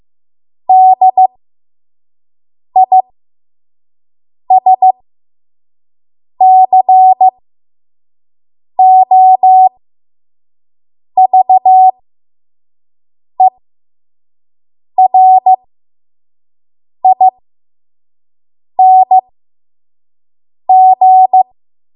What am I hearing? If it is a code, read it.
Morse code, heard as a keyed tone, about 15 wpm, spaced as for 5 wpm, 750 Hz. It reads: DISCOVERING
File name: Morse / 05WPM